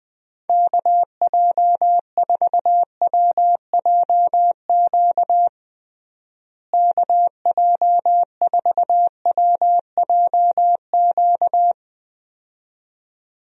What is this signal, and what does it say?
Morse code, heard as a keyed tone, about 20 wpm, 700 Hz. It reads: KJ4WJQ KJ4WJQ
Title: USA Call Signs - 6 Characters - Repeated 2x - 20wpm